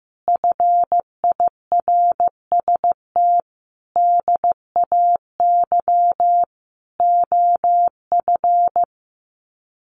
Morse code, heard as a keyed tone, about 15 wpm, 700 Hz. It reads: FIRST DAY OF
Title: Sets of 5 Words Encore 15wpm